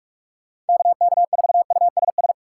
Morse code, heard as a keyed tone, about 46 wpm, 700 Hz. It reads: KK4USS